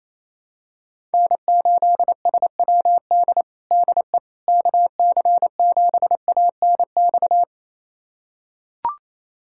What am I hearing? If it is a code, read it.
Morse code, heard as a keyed tone, about 28 wpm, 700 Hz. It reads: N8SWD DE KC7ANX
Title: USA Callsign DE Callsign 28wpm